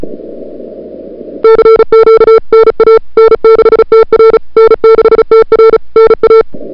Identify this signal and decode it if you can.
Morse code, heard as a keyed tone, about 35 wpm, 440 Hz. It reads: CQ NA N6TR N6TR NA
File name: CWSprint / February 2007 CW Sprint / 01-2007 CW Sprint Hour1